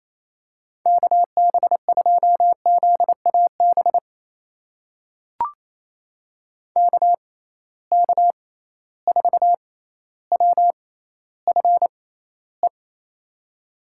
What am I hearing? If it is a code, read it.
Morse code, heard as a keyed tone, about 28 wpm, spaced as for 14 wpm, 700 Hz. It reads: KB2ZAB KK4WFE
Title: USA Call Signs - 6 Characters - 6x Character Spacing - 28wpm